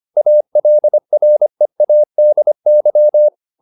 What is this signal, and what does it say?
Morse code, heard as a keyed tone, about 25 wpm, 600 Hz. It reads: ALREADY